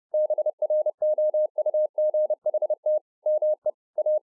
Morse code, heard as a keyed tone, about 30 wpm, 610 Hz. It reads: BROUGHT ME A